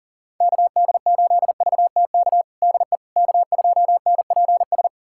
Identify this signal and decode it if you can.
Morse code, heard as a keyed tone, about 37 wpm, 700 Hz. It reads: KD8VTK DE K2NPS
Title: USA Callsign DE Callsign 40wpm